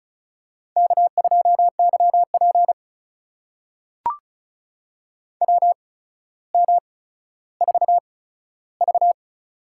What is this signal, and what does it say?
Morse code, heard as a keyed tone, about 36 wpm, spaced as for 15 wpm, 700 Hz. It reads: K2YP WM4V
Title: USA Call Signs - 4 Characters - 8x Character Spacing - 35wpm